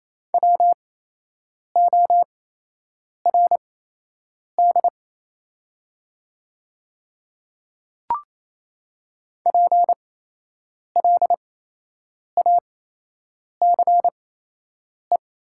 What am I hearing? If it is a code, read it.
Morse code, heard as a keyed tone, about 28 wpm, spaced as for 8 wpm, 700 Hz. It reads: WORD PLACE
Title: 8x Character Spacing - Top 300 Words - 28wpm